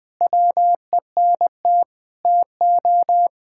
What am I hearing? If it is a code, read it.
Morse code, heard as a keyed tone, about 20 wpm, 700 Hz. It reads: WENT TO
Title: Sentences with 4-Word Phrases - 20wpm